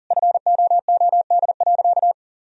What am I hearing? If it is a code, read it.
Morse code, heard as a keyed tone, about 37 wpm, 700 Hz. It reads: FOOD.
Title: Sentences from Top 1000 Words - Review Part 1 - 40wpm